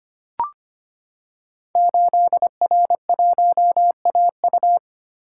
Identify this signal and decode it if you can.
Morse code, heard as a keyed tone, about 25 wpm, 700 Hz. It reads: 8R1AU
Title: Code Groups - 5 Letter:Numbers - 25wpm